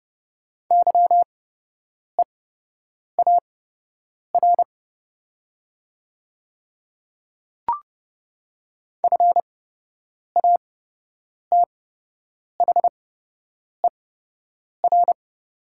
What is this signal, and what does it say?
Morse code, heard as a keyed tone, about 30 wpm, spaced as for 8 wpm, 700 Hz. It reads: YEAR FATHER
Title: 8x Character Spacing - Top 200 Words - 30wpm